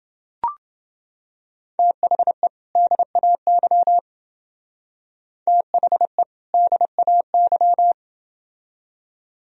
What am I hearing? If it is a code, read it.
Morse code, heard as a keyed tone, about 29 wpm, 700 Hz. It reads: THE DAY THE DAY